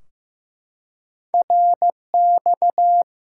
Morse code, heard as a keyed tone, about 15 wpm, 700 Hz. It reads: RX
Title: Top 90 QSO Elements 15wpm